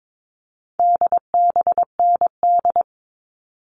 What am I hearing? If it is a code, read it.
Morse code, heard as a keyed tone, about 22 wpm, 700 Hz. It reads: DBND
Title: All Cut Numbers - 4-Digit 22wpm